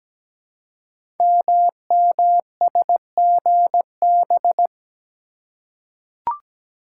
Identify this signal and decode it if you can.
Morse code, heard as a keyed tone, about 17 wpm, 700 Hz. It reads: MMSGB
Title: Code Groups - 5 Letters - 17wpm